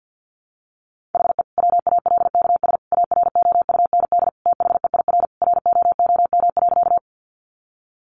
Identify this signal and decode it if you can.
Morse code, heard as a keyed tone, about 48 wpm, 700 Hz. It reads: HE WALKS AROUND THEIR ROOM.
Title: Easy Phrases 50wpm